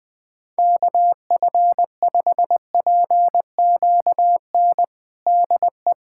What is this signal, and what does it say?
Morse code, heard as a keyed tone, about 20 wpm, 700 Hz. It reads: KF5PQN DE
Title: USA Callsign DE Callsign - 6 Characters - 20wpm